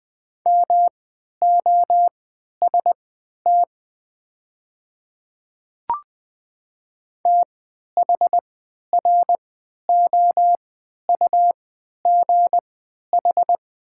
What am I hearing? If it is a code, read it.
Morse code, heard as a keyed tone, about 20 wpm, spaced as for 11 wpm, 700 Hz. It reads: MOST THROUGH
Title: 3x Character Spacing - Top 200 Words - 20wpm